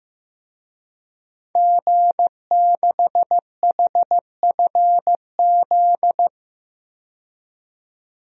Morse code, heard as a keyed tone, about 15 wpm, 700 Hz. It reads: G6HFZ
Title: DXCC-100 Call Signs - 5 Characters - 15wpm